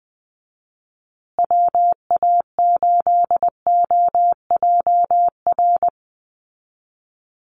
Morse code, heard as a keyed tone, about 20 wpm, 700 Hz. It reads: WA8OJR
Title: Call Signs Repeated 2x - 20wpm